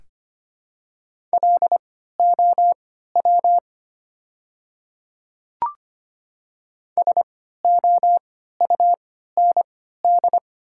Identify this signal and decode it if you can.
Morse code, heard as a keyed tone, about 25 wpm, spaced as for 14 wpm, 700 Hz. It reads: LOW SOUND
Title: 3x Character Spacing - Top 200 Words - 25wpm